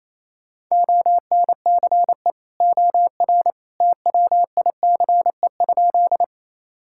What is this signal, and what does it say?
Morse code, heard as a keyed tone, about 28 wpm, 700 Hz. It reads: ONCE OR TWICE?
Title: Sentences from Top 1600 Words 28wpm